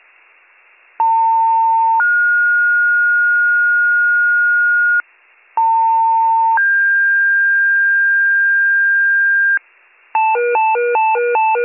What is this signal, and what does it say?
Two-tone page, heard as a one-second tone, then a three-second tone, about 900 then 1430 Hz, about 900 then 1600 Hz, with noise. It is heard from 1.0 to 9.6 s.